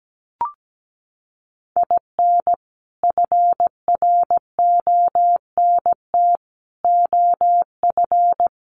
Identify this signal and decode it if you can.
Morse code, heard as a keyed tone, about 17 wpm, 700 Hz. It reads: IN FRONT OF
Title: Sets of 4 Words 17wpm